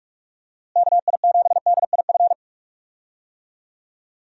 Morse code, heard as a keyed tone, about 44 wpm, 700 Hz. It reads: KI7DIF
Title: USA Call Signs - 6 Characters - Repeated 3x - 45wpm